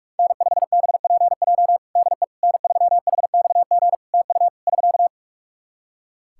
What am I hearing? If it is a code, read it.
Morse code, heard as a keyed tone, about 45 wpm, 700 Hz. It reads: N5BPJ DE N3HXG TU <SK>
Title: K1USN SST QSO Practice 45wpm